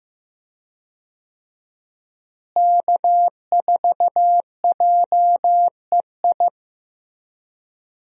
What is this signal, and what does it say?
Morse code, heard as a keyed tone, about 16 wpm, 700 Hz. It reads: K4JEI
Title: USA Call Signs - 5 Characters - Repeated 2x - 15wpm